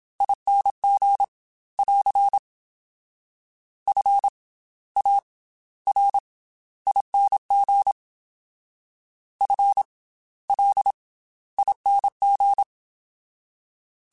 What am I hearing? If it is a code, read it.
Morse code, heard as a keyed tone, about 27 wpm, 800 Hz. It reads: ING <AR> F A R ING F L ING
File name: Special INGs / Special INGs-mp3s / Slow ing @25